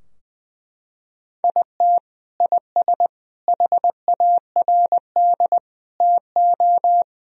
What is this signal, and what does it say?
Morse code, heard as a keyed tone, about 21 wpm, 700 Hz. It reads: IT IS HARD TO
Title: Sentences from Top 1000 Words - Review Part 3 - 20wpm